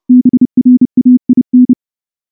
Morse code, heard as a keyed tone, about 28 wpm, 250 Hz. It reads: BRAIN
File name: morse / morse-training-30WPM